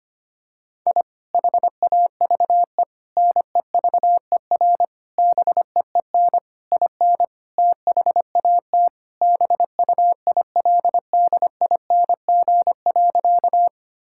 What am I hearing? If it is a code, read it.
Morse code, heard as a keyed tone, about 25 wpm, 700 Hz. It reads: I HAVE NEVER BEEN IN THAT BUILDING.
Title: Sentences from Top 1000 Words - Review Part 5 - 25wpm